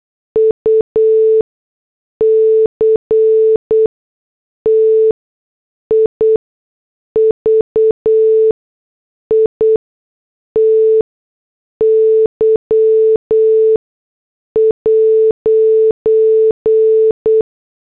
Morse code, heard as a keyed tone, about 8 wpm, spaced as for 6 wpm, 440 Hz. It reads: UCTIVITY'
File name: morse / headlines